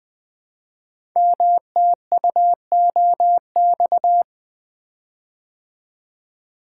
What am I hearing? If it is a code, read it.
Morse code, heard as a keyed tone, about 20 wpm, 700 Hz. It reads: MTUOX
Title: Code Groups - 5 Letters - 20wpm